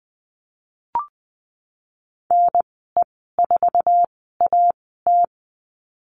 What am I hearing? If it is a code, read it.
Morse code, heard as a keyed tone, about 19 wpm, spaced as for 14 wpm, 700 Hz. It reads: NE4AT
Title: 055 Lesson 10g - Letter r - Call Signs 20wpm